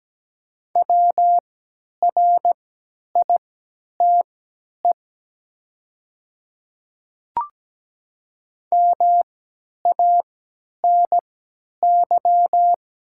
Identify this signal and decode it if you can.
Morse code, heard as a keyed tone, about 17 wpm, spaced as for 10 wpm, 690 Hz. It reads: WRITE MANY
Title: 3x Character Spacing - Top 100 Words - 17wpm